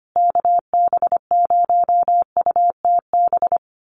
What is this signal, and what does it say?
Morse code, heard as a keyed tone, about 25 wpm, 700 Hz. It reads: KB0UTB
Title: USA Call Signs Repeated 3x 25wpm